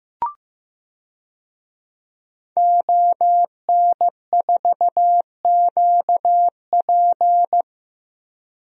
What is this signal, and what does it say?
Morse code, heard as a keyed tone, about 15 wpm, 700 Hz. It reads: ON4QP
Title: DXCC-100 Call Signs - 5 Characters - Repeated 2x - 15wpm